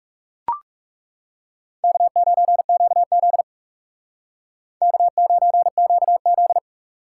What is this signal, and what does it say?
Morse code, heard as a keyed tone, about 38 wpm, 700 Hz. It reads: K9QZ K9QZ